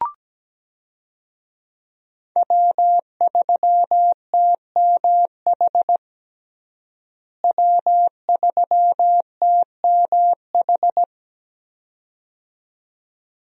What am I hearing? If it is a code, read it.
Morse code, heard as a keyed tone, about 17 wpm, 700 Hz. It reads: W3TMH W3TMH
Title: USA Call Signs - 5 Characters - Repeated 2x - 17wpm